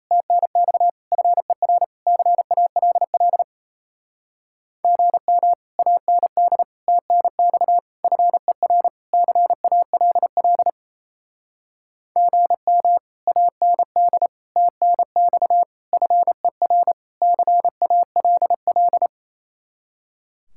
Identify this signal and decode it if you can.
Morse code, heard as a keyed tone, about 37 wpm, 700 Hz. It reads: TNX FER CALL GM AND TNX FER CALL GM AND TNX FER CALL